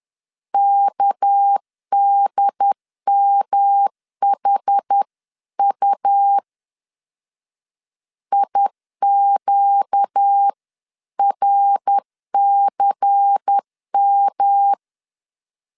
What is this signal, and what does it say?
Morse code, heard as a keyed tone, about 10 wpm, 800 Hz. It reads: KDMHU IQRCM
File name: LEZIONI / 6 PKX0 / 6 PK 10wpm 2436